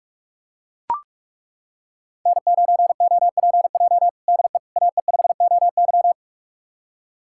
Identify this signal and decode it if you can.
Morse code, heard as a keyed tone, about 41 wpm, 700 Hz. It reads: N9OPJ DE AE5OY
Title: USA Callsign DE Callsign - 5 Characters - 45wpm